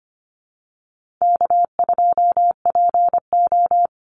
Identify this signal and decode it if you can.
Morse code, heard as a keyed tone, about 25 wpm, 700 Hz. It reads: K2PO
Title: Easy Call Signs - 25wpm